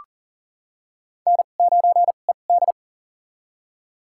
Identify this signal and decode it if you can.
Morse code, heard as a keyed tone, about 40 wpm, spaced as for 29 wpm, 700 Hz. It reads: N9ED
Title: USA Call Signs - 2x Character Spacing - 40wpm